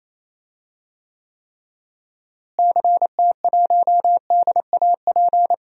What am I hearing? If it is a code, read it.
Morse code, heard as a keyed tone, about 28 wpm, 700 Hz. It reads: CT1DAP
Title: DXCC-100 Call Signs - 6 Characters - Repeated 3x - 28wpm